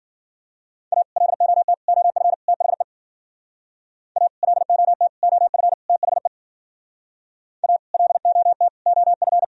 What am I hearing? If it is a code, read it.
Morse code, heard as a keyed tone, about 50 wpm, 700 Hz. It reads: A LOT OF THE A LOT OF THE A LOT OF
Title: Speed-Racing - Sets of 4 Words 40wpm